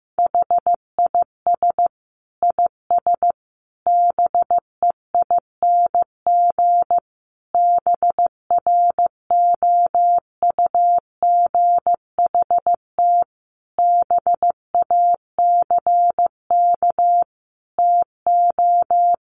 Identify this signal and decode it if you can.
Morse code, heard as a keyed tone, about 15 wpm, 700 Hz. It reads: HIS IS BEING BROUGHT BACK TO
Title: Sentences from Top 600 Words 15wpm